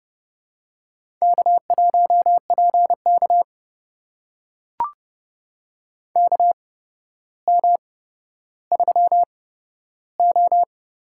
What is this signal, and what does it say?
Morse code, heard as a keyed tone, about 30 wpm, spaced as for 13 wpm, 700 Hz. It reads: K1PK KM3O
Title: USA Call Signs - 4 Characters - 8x Character Spacing - 30wpm